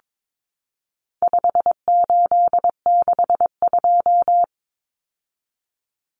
Morse code, heard as a keyed tone, about 22 wpm, 700 Hz. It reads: 5862
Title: Common Cut Numbers - 4-Digit 22wpm